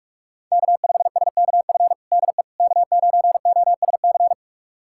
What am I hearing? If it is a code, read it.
Morse code, heard as a keyed tone, about 46 wpm, 700 Hz. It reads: K5SKF DE K9OSC
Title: USA Callsign DE Callsign - 5 Characters - 45wpm